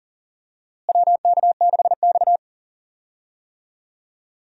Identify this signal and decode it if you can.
Morse code, heard as a keyed tone, about 40 wpm, 700 Hz. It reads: WK6X
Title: USA Call Signs - 4 Characters - Repeated 2x - 40wpm